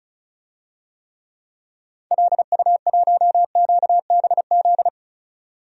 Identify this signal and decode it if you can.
Morse code, heard as a keyed tone, about 34 wpm, 700 Hz. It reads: LU1QBZ
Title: DXCC-100 Call Signs - 6 Characters - 35wpm